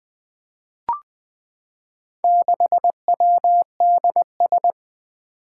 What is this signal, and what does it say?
Morse code, heard as a keyed tone, about 20 wpm, 700 Hz. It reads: 6WDS
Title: Four Challenging Letter-Number Combinations 20wpm